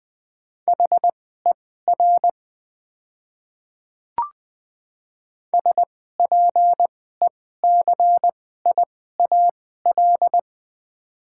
Morse code, heard as a keyed tone, about 20 wpm, spaced as for 14 wpm, 700 Hz. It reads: HER SPECIAL
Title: 2x Character Spacing - Top 500 Words - 20wpm